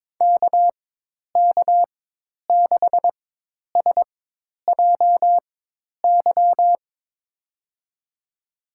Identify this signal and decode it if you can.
Morse code, heard as a keyed tone, about 21 wpm, spaced as for 10 wpm, 700 Hz. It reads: KK6SJY
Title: USA Call Signs - 6 Characters - 4x Character Spacing - 22wpm